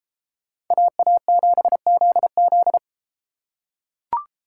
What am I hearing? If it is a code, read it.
Morse code, heard as a keyed tone, about 29 wpm, 700 Hz. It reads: AA7ZZ